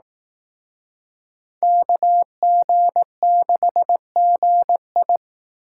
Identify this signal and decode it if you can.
Morse code, heard as a keyed tone, about 18 wpm, 700 Hz. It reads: KG6GI
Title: Speed Racing - USA 2x2 Call Signs 15wpm